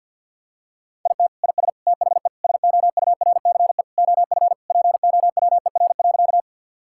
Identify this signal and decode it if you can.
Morse code, heard as a keyed tone, about 47 wpm, 700 Hz. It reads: IT IS THE SOURCE OF POWER.
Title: Sentences from Top 1000 Words - Review Part 5 - 50wpm